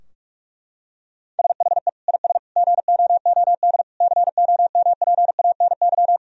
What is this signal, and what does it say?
Morse code, heard as a keyed tone, about 46 wpm, 700 Hz. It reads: SHE IS GOOD COMPANY
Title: Sentences from Top 1000 Words - Review Part 3 - 45wpm